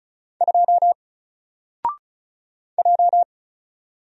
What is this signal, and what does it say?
Morse code, heard as a keyed tone, about 36 wpm, 700 Hz. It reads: 2 J
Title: Single Letter-Number Alt 35wpm